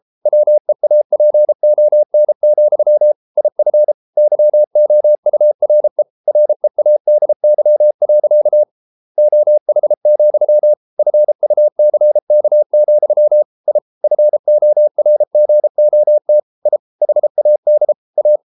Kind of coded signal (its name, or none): Morse code